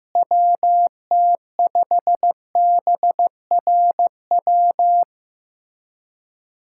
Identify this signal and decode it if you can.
Morse code, heard as a keyed tone, about 15 wpm, 700 Hz. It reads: WT5BRW
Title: USA Callsign DE Callsign 15wpm